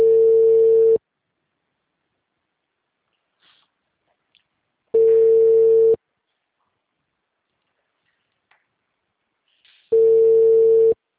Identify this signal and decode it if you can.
Ringback tone (China, 450 Hz).
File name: out- -2035- - - .24655